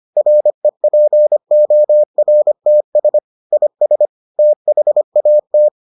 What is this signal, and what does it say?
Morse code, heard as a keyed tone, about 25 wpm, 600 Hz. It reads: REPORTS IS THAT